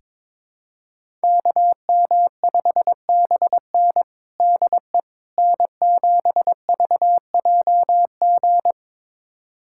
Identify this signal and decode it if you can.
Morse code, heard as a keyed tone, about 21 wpm, 700 Hz. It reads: KM5BN DE N7VJG
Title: USA Callsign DE Callsign - 5 Characters - 22wpm